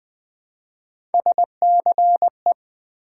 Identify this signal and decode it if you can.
Morse code, heard as a keyed tone, about 20 wpm, 700 Hz. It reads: SCE
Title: Build A Word - 1 Letter At A Time - 3 to 5-Letter Words - 20wpm